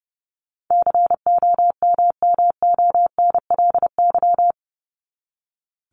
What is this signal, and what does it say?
Morse code, heard as a keyed tone, about 30 wpm, 700 Hz. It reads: COMMONLY
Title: LY Suffix - Top 100 Words with Base Words 30wpm